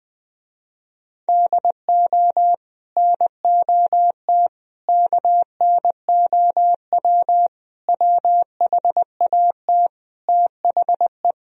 Morse code, heard as a keyed tone, about 21 wpm, 700 Hz. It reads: DO NOT KNOW WHAT THE
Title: Speed-Racing - Sets of 5 Words 20wpm